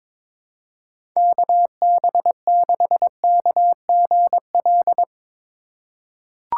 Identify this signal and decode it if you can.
Morse code, heard as a keyed tone, about 22 wpm, 700 Hz. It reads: KB6KGL
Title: USA Call Signs - 6 Characters - 4x Character Spacing - 22wpm